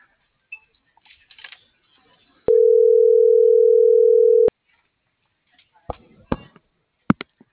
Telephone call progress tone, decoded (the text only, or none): ringback tone